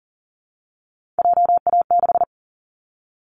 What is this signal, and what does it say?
Morse code, heard as a keyed tone, about 40 wpm, 700 Hz. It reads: JA6
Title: Callsign Prefixes for Top 25 DXCC - 40wpm